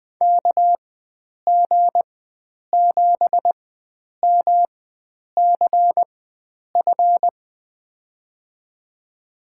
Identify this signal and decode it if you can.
Morse code, heard as a keyed tone, about 20 wpm, spaced as for 9 wpm, 700 Hz. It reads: KG7MCF
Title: USA Call Signs - 6 Characters - 4x Character Spacing - 20wpm